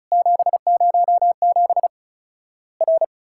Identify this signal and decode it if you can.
Morse code, heard as a keyed tone, about 36 wpm, 700 Hz. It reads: 707 R
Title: Worked All Europe DX Contest - QTC Exchange - 35wpm